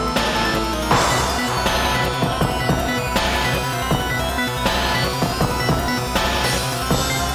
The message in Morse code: none